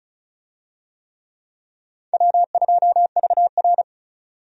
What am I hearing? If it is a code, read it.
Morse code, heard as a keyed tone, about 36 wpm, 700 Hz. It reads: W2VR
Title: USA Call Signs - 4 Characters - 35wpm